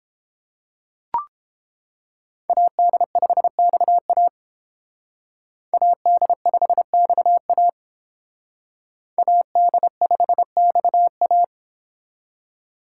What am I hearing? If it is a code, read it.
Morse code, heard as a keyed tone, about 28 wpm, 700 Hz. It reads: AD5XA AD5XA AD5XA